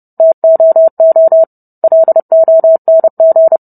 Morse code, heard as a keyed tone, about 30 wpm, 650 Hz. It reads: TOO LONG